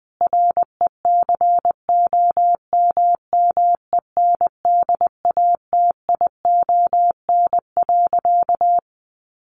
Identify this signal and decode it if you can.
Morse code, heard as a keyed tone, about 20 wpm, 700 Hz. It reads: RECOMMENDATION.
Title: Sentences from Top 1400 Words 20wpm